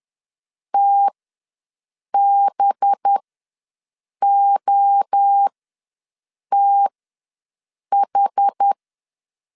Morse code, heard as a keyed tone, about 10 wpm, spaced as for 6 wpm, 800 Hz. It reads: TBOTH
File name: LEZIONI / 5 LBVC / 5 LBVC ws 10wpm 3733